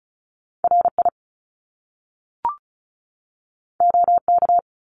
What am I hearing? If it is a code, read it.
Morse code, heard as a keyed tone, about 36 wpm, 700 Hz. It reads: RI OK
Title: Top Two Letter Combinations in Words 35wpm